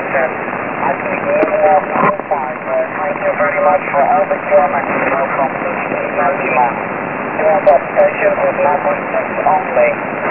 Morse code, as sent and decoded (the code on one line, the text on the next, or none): none